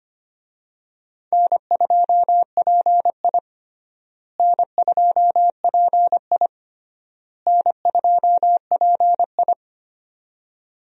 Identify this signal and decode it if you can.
Morse code, heard as a keyed tone, about 25 wpm, 700 Hz. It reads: N2PI N2PI N2PI
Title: USA Call Signs - 4 Characters - Repeated 3x - 25wpm